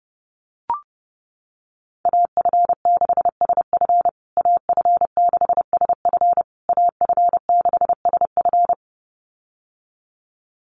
Morse code, heard as a keyed tone, about 28 wpm, 700 Hz. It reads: AF6SF AF6SF AF6SF
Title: USA Call Signs Repeated 3x 30wpm